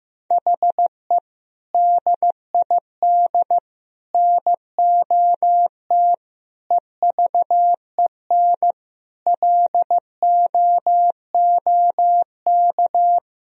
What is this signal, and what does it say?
Morse code, heard as a keyed tone, about 16 wpm, 700 Hz. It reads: HE DID NOT EVEN LOOK
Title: Sentences from Top 1000 Words - Review Part 1 - 15wpm